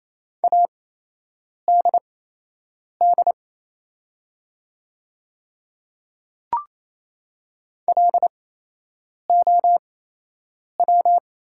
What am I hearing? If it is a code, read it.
Morse code, heard as a keyed tone, about 28 wpm, spaced as for 8 wpm, 700 Hz. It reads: ADD LOW
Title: 8x Character Spacing - Top 300 Words - 28wpm